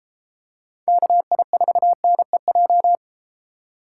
Morse code, heard as a keyed tone, about 32 wpm, 700 Hz. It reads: KI4NEJ